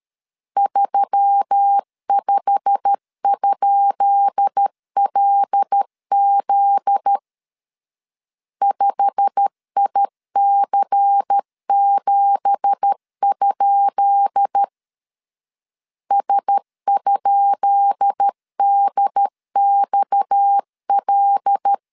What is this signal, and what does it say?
Morse code, heard as a keyed tone, about 13 wpm, 800 Hz. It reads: 35?LZ 5IC7? S?DXL
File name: LEZIONI / 10 SIGNS / 10B INT 12wpm 2414